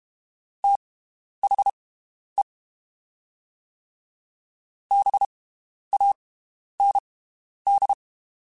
Morse code, heard as a keyed tone, about 32 wpm, spaced as for 11 wpm, 800 Hz. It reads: THE BAND
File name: (spaced) / QSOs Spaced-mp3s / WD0BGP 1 @30 sp